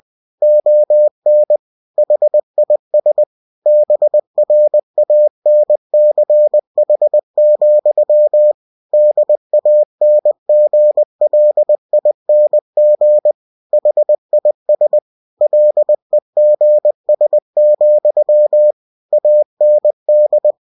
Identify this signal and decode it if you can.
Morse code, heard as a keyed tone, about 20 wpm, 600 Hz. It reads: ON HIS BRANCH, DANGLING HIS LEGS, AND
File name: cw-books / winnie-the-pooh / winnie-20wpm-page-0094